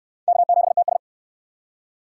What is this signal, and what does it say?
Morse code, heard as a keyed tone, about 68 wpm, 700 Hz. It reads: K7TS